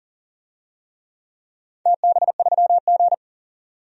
Morse code, heard as a keyed tone, about 40 wpm, 700 Hz. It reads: TB3G
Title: DXCC-100 Call Signs - 4 Characters - 40wpm